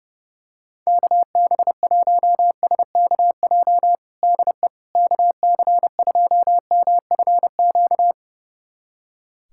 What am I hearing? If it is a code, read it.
Morse code, heard as a keyed tone, about 30 wpm, 700 Hz. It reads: KB1SKJ DE KC2MFQ
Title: USA Callsign DE Callsign 30wpm